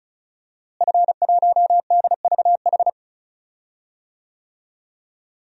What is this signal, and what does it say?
Morse code, heard as a keyed tone, about 36 wpm, 700 Hz. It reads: F1DVH
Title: DXCC-100 Call Signs - 5 Characters - Repeated 2x - 35wpm